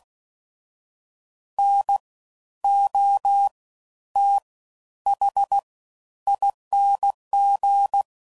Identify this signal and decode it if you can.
Morse code, heard as a keyed tone, about 16 wpm, spaced as for 11 wpm, 800 Hz. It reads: NOTHING
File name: Suffixes / Special INGs / Fast Ings